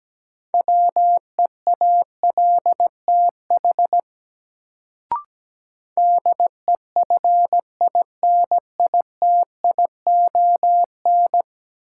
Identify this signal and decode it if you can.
Morse code, heard as a keyed tone, about 17 wpm, 700 Hz. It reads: WEALTH DEFINITION
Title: Sentences from Top 1000 Words - Review Part 4 - New Word Drill - 17wpm